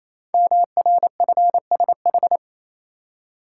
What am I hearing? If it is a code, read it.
Morse code, heard as a keyed tone, about 28 wpm, 700 Hz. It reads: MRFSH
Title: Code Groups - 5 Letters - 28wpm